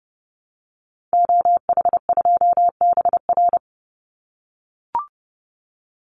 Morse code, heard as a keyed tone, about 30 wpm, 700 Hz. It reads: OH2BR
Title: Call Signs - 30wpm